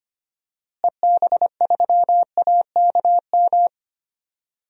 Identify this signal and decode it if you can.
Morse code, heard as a keyed tone, about 25 wpm, 700 Hz. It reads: EB3AKM
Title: DXCC-100 Call Signs - 6 Characters - Repeated 3x - 25wpm